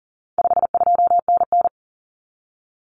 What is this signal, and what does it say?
Morse code, heard as a keyed tone, about 40 wpm, 700 Hz. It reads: 52NN
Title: Common Cut Numbers - 4-Digit 40wpm